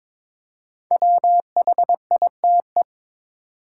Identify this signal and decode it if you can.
Morse code, heard as a keyed tone, about 22 wpm, 700 Hz. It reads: WHITE
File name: 5 Letter Words - First 3 Letters Given - 22wpm